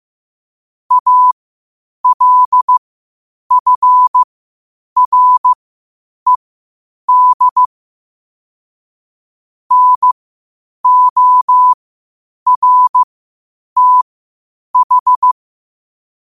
Morse code, heard as a keyed tone, about 15 wpm, spaced as for 8 wpm, 1000 Hz. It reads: ALFRED NORTH
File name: qotdfiles / 8wpm